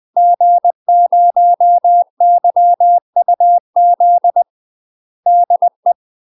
Morse code, heard as a keyed tone, about 20 wpm, 700 Hz. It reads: G0YUZ DE
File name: UK MASTER QSOs TRAINING / 20WPM / UK MASTER QSOs 0044